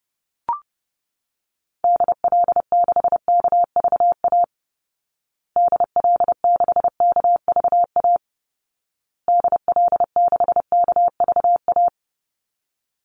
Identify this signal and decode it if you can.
Morse code, heard as a keyed tone, about 28 wpm, 700 Hz. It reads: DL6KVA DL6KVA DL6KVA
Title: Call Signs Repeated 2x - 30wpm